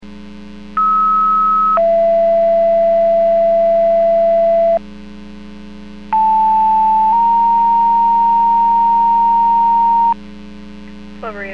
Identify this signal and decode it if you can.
Two-tone page, heard as a one-second tone, then a three-second tone, about 1290 then 670 Hz, about 880 then 930 Hz. It is heard from 0.8 to 10.1 s.
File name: Portage 154.385 Paging